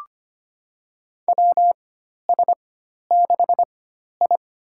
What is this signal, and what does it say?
Morse code, heard as a keyed tone, about 25 wpm, spaced as for 12 wpm, 700 Hz. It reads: WS6I
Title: USA Call Signs - 4 Characters - 4x Character Spacing - 25wpm